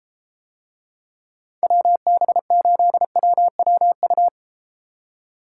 Morse code, heard as a keyed tone, about 33 wpm, 700 Hz. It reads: WB8WWU